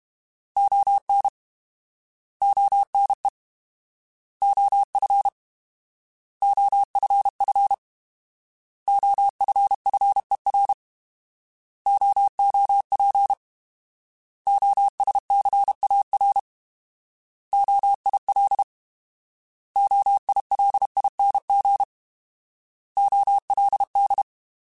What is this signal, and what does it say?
Morse code, heard as a keyed tone, about 32 wpm, 800 Hz. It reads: ON ONE OF OFF OFFER OOP OSCAR OIL OILING OLD
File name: Alphabetical / Alphabetical mp3s / Os @30